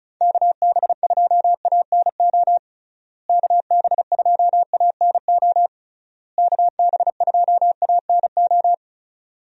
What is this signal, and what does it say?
Morse code, heard as a keyed tone, about 34 wpm, 700 Hz. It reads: KB2ANO KB2ANO KB2ANO
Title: USA Call Signs - 6 Characters - Repeated 3x - 35wpm